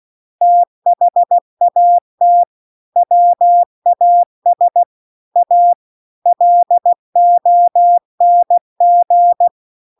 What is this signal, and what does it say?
Morse code, heard as a keyed tone, about 16 wpm, 700 Hz. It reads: THAT WAS A LONG